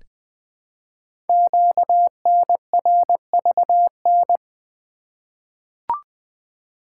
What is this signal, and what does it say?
Morse code, heard as a keyed tone, about 20 wpm, 700 Hz. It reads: QNRVN
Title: Code Groups - 5 Letters - 20wpm